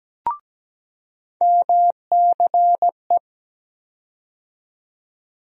Morse code, heard as a keyed tone, about 17 wpm, 700 Hz. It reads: MCE
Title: USA Call Sign Postfixes - 3 Letters 17wpm